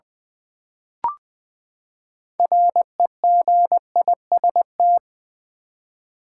Morse code, heard as a keyed tone, about 20 wpm, 700 Hz. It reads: REGIST
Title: Build A Word - 6 Letters At A Time - 12-Letter Words - 20wpm